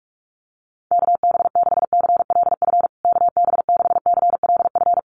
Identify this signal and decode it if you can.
Morse code, heard as a keyed tone, about 44 wpm, 700 Hz. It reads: KB6CLF KB6CLF
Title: USA Call Signs Repeated 3x 45wpm